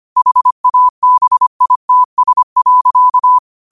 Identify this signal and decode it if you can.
Morse code, heard as a keyed tone, about 25 wpm, 1000 Hz. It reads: HABITS.
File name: qotdfiles / 25wpm